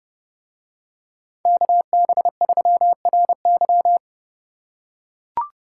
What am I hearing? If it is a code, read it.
Morse code, heard as a keyed tone, about 28 wpm, 700 Hz. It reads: KB3RY